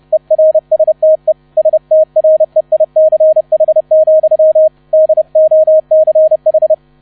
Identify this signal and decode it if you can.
Morse code, heard as a keyed tone, about 30 wpm, 630 Hz, with radio noise. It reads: ERSTE STREICH, DOCH